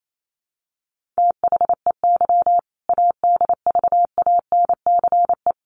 Morse code, heard as a keyed tone, about 28 wpm, 700 Hz. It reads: THEY ADVANCE
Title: Sentences from Top 1400 Words 28wpm